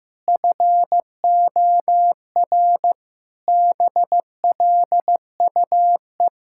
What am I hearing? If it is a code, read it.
Morse code, heard as a keyed tone, about 15 wpm, 700 Hz. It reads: FOR BLUE
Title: Pangram Challenge v2 - 15wpm